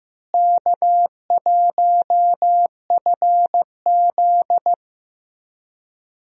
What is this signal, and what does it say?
Morse code, heard as a keyed tone, about 15 wpm, 700 Hz. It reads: K1FZ
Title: USA Call Signs - 4 Characters - Repeated 3x - 15wpm